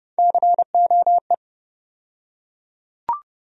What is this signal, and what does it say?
Morse code, heard as a keyed tone, about 31 wpm, 700 Hz. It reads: COE
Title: USA Call Sign Postfixes - 3 Letters 30wpm